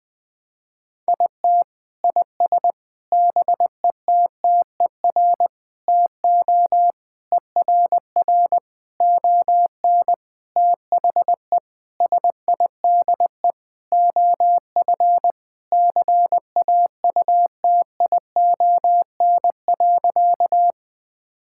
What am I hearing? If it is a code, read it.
Morse code, heard as a keyed tone, about 20 wpm, 700 Hz. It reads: IT IS BETTER TO ERR ON THE SIDE OF CAUTION.
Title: Sentences with 7-Word Phrases - 20wpm